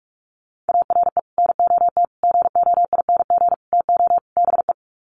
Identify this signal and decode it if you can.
Morse code, heard as a keyed tone, about 44 wpm, 700 Hz. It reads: ARE NOT GOING TO BE
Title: Sets of 5 Words Encore 45wpm